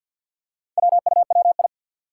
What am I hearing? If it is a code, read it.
Morse code, heard as a keyed tone, about 49 wpm, 700 Hz. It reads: WUWI
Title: Four Challenging Letter-Number Combinations 50wpm